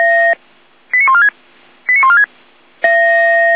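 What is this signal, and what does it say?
Five-tone selective call (ZVEI), heard as quick runs of five tones, tones of about 70 ms. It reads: 89146 89146
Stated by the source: Five-tone ZVEI selective call 89299, received by SDR.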